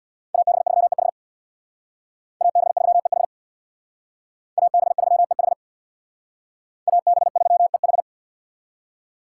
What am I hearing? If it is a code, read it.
Morse code, heard as a keyed tone, about 61 wpm, 700 Hz. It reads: AB3EH AB3EH AB3EH AB3EH
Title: Speed Racing - USA 2x2 Call Signs 50wpm